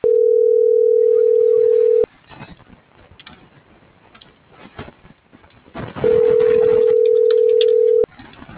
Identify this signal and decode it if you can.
Ringback tone (North America), 440 + 480 Hz.